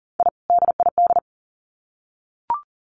Morse code, heard as a keyed tone, about 40 wpm, 700 Hz. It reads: I DID
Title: Sets of 2 Words 40wpm